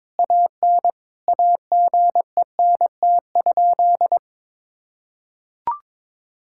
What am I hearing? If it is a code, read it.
Morse code, heard as a keyed tone, about 22 wpm, 700 Hz. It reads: AN AGENT?
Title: Sentences from Top 1600 Words 22wpm